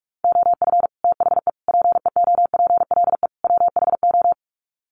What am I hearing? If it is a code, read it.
Morse code, heard as a keyed tone, about 44 wpm, 700 Hz. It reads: OF THE PEOPLE WHO
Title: Sets of 4 Words 45wpm